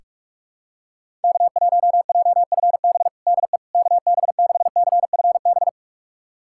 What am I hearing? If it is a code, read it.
Morse code, heard as a keyed tone, about 42 wpm, 700 Hz. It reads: K1JFB DE KB6CFB
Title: USA Callsign DE Callsign 45wpm